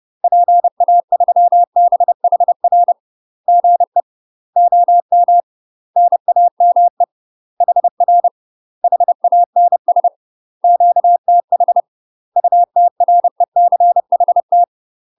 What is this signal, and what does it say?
Morse code, heard as a keyed tone, about 30 wpm, 700 Hz. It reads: PA3BHR GE OM NAME HR HANS QTH UTRECHT
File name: UK MASTER QSOs TRAINING / 30WPM / UK MASTER QSOs 0018